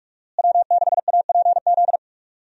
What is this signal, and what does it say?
Morse code, heard as a keyed tone, about 46 wpm, 700 Hz. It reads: W6APZ
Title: USA Call Signs - 5 Characters - Repeated 3x - 45wpm